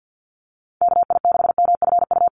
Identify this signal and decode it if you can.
Morse code, heard as a keyed tone, about 49 wpm, 700 Hz. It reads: KI6MFU
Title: USA Call Signs Repeated 3x 50wpm